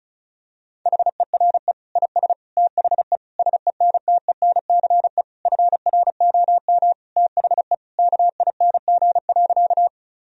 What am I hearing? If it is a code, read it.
Morse code, heard as a keyed tone, about 36 wpm, 700 Hz. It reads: HERE IS THE SENTENCE FROM THE KING.
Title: Sentences from Top 1000 Words - Review Part 1 - 35wpm